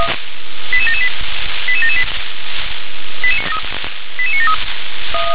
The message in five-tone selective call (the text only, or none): 88778 88778 88973